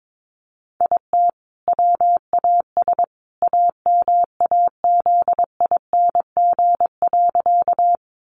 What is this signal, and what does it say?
Morse code, heard as a keyed tone, about 22 wpm, 700 Hz. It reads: IT WAS AMAZING.
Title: Sentences from Top 1500 Words 22wpm